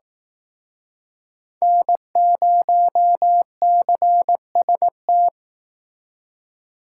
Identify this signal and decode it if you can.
Morse code, heard as a keyed tone, about 18 wpm, 700 Hz. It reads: N0CST
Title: Speed Racing - USA 1x3 Call Signs 15wpm